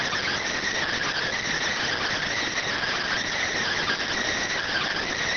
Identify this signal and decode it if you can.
Emergency Alert System (SAME) tones, heard as harsh buzzing bursts. It is heard from 1.0 to 4.9 s.